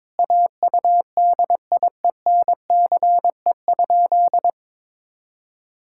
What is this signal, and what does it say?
Morse code, heard as a keyed tone, about 22 wpm, 700 Hz. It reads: AUDIENCE?
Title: Sentences from Top 1000 Words - Review Part 5 - 22wpm